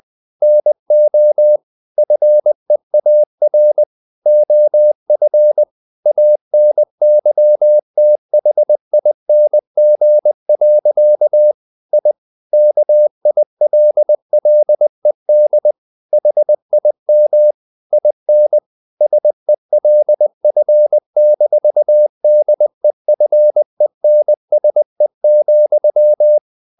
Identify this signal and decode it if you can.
Morse code, heard as a keyed tone, about 20 wpm, 600 Hz. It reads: NO FEAR OF ANYTHING. I KILLED HIM IN SELF-DEFENSE,
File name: the old man and the sea part 2 20wpm